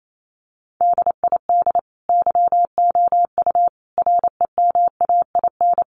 Morse code, heard as a keyed tone, about 28 wpm, 700 Hz. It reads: DID YOU REMAIN
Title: Sentences from Top 1200 Words 28wpm